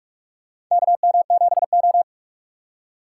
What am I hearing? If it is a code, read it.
Morse code, heard as a keyed tone, about 46 wpm, 700 Hz. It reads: KM7O
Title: USA Call Signs - 4 Characters - 4x Character Spacing - 45wpm